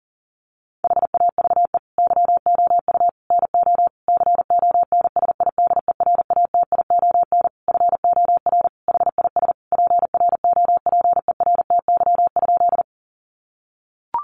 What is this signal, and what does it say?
Morse code, heard as a keyed tone, about 40 wpm, 700 Hz. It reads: HAVE YOU NO CONSIDERATION FOR HIS PROPERTY?
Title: Sentences from Top 1500 Words 40wpm